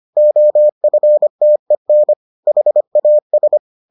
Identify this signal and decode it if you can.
Morse code, heard as a keyed tone, about 25 wpm, 600 Hz. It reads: OFTEN HAS